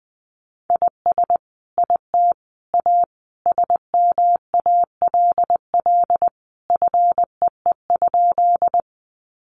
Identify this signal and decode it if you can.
Morse code, heard as a keyed tone, about 20 wpm, 700 Hz. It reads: IS IT A SMALL FEE?
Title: Sentences from Top 1500 Words 20wpm